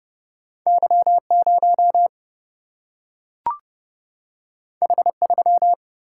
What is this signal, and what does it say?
Morse code, heard as a keyed tone, about 31 wpm, 700 Hz. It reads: Y0 H3
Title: Single-Letter Single-Number 30wpm